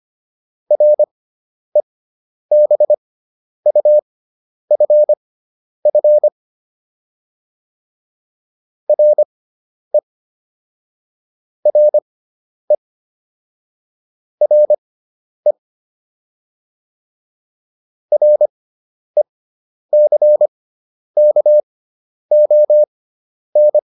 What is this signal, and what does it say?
Morse code, heard as a keyed tone, about 25 wpm, spaced as for 10 wpm, 600 Hz. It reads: REBUFF RE RE RE RECKON
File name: RE6 10